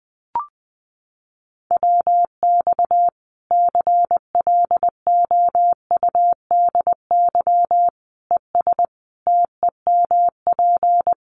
Morse code, heard as a keyed tone, about 20 wpm, 700 Hz. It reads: WX CLOUDY ES TEMP
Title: Common QSO Phrases 20wpm